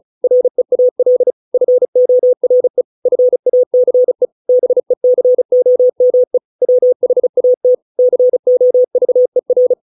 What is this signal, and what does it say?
Morse code, heard as a keyed tone, about 35 wpm, 500 Hz. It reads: REAL FORE FACE BECOME WHAT COVER